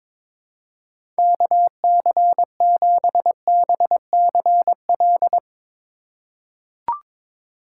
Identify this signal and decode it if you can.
Morse code, heard as a keyed tone, about 22 wpm, 700 Hz. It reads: KC7BCL
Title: USA Call Signs - 6 Characters - 8x Character Spacing - 22wpm